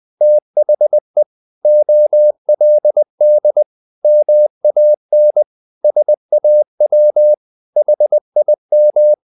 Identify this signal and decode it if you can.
Morse code, heard as a keyed tone, about 20 wpm, 600 Hz. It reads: THE OLD MAN SAW HIM
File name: the old man and the sea part 2 20wpm